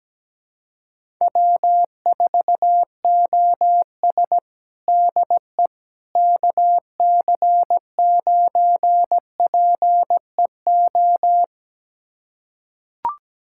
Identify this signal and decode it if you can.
Morse code, heard as a keyed tone, about 17 wpm, 700 Hz. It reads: W4OS DE KC9PEO
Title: USA Callsign DE Callsign 17wpm